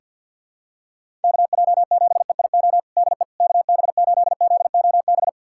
Morse code, heard as a keyed tone, about 46 wpm, 700 Hz. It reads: KJ7EIO DE KB8ZOB
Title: USA Callsign DE Callsign 50wpm